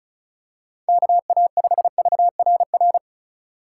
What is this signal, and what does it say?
Morse code, heard as a keyed tone, about 34 wpm, 700 Hz. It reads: KA5VRR
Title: USA Call Signs - 6 Characters - 2x Character Spacing - 35wpm